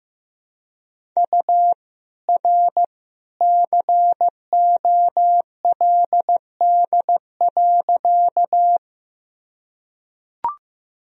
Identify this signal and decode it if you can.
Morse code, heard as a keyed tone, about 15 wpm, 700 Hz. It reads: U R COLD.
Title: Sentences from Top 400 Words and Abbreviations 15wpm